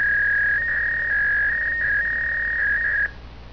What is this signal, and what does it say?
Fax machine handshake.